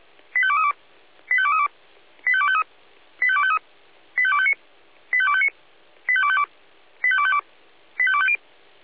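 Five-tone selective call (ZVEI), heard as quick runs of five tones, tones of about 80 ms. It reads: 85232 85232 85253 85253 85259 85259 85252 85252 85250
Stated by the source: ZVEI five-tone selective call received on SDR.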